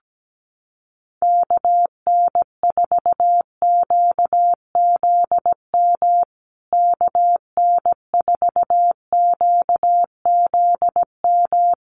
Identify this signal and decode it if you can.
Morse code, heard as a keyed tone, about 17 wpm, 700 Hz. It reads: KN4QZM KN4QZM
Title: USA Call Signs Repeated 3x 17wpm